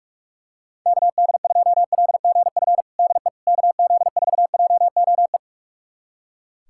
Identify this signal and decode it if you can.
Morse code, heard as a keyed tone, about 43 wpm, 700 Hz. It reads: KD2LGF DE KZ4JOE
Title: USA Callsign DE Callsign - 6 Characters - 45wpm